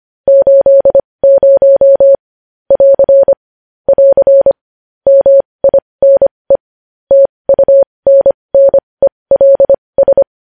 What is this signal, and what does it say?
Morse code, heard as a keyed tone, about 25 wpm, spaced as for 20 wpm, 550 Hz. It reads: 80 <AR> <AR> MINE TUNNELS
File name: podcast / News headlines-1x-25x20